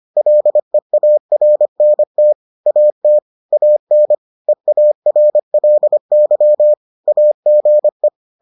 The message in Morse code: LEARNT AT AN EARLY AGE

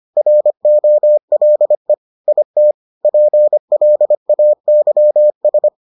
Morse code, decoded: ROLE IT PLAYS

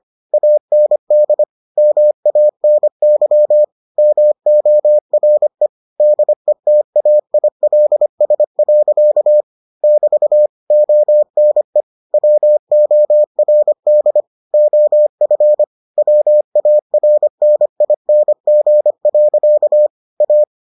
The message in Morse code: AND MANY MORE DETAILS. <BT> ONE WORD OF WARNING. A

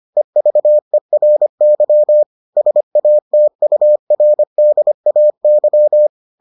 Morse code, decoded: EVERY SATURDAY

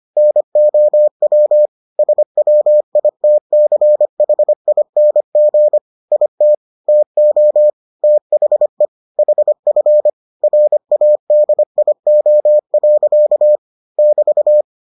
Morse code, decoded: NOW SWITCHING IT TO THE HF RADIO. <BT>

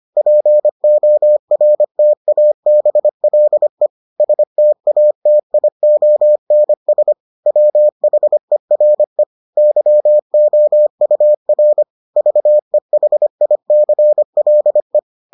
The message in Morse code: PORTABLE STATIONS WHERE YOUR VEHICLE

